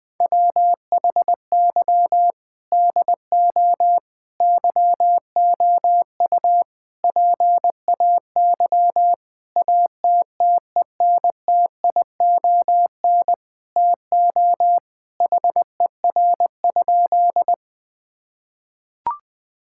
Morse code, heard as a keyed tone, about 21 wpm, 700 Hz. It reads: WHY DO YOU PAY ATTENTION TO HER?
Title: Sentences from Top 1000 Words - Review Part 6 - 20wpm